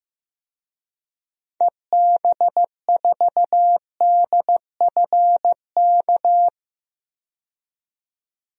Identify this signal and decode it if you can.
Morse code, heard as a keyed tone, about 15 wpm, 700 Hz. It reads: EB4DFK
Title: DXCC-100 Call Signs - 6 Characters - 15wpm